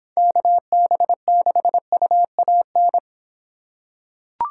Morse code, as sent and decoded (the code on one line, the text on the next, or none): -.- -... -.... ..- .- -.
KB6UAN